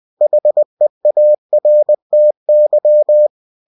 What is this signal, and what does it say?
Morse code, heard as a keyed tone, about 20 wpm, 600 Hz. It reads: HEARTY
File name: cw-books / winnie-the-pooh / winnie-20wpm-page-0104